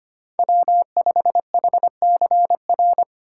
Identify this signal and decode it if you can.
Morse code, heard as a keyed tone, about 25 wpm, 700 Hz. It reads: W5HCR
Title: USA Call Signs - 5 Characters - 4x Character Spacing - 25wpm